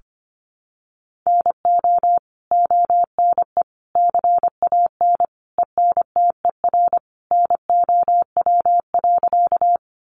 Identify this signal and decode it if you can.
Morse code, heard as a keyed tone, about 25 wpm, 700 Hz. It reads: NO ONE CAN ENTER NOW.
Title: Sentences from Top 1300 Words 25wpm